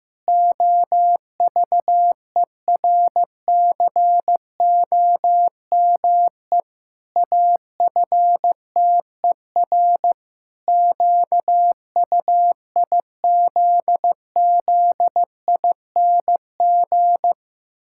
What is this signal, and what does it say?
Morse code, heard as a keyed tone, about 14 wpm, 700 Hz. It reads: OVERCOME AFTER QUIZZING
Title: Pangram Challenge v2 - 15wpm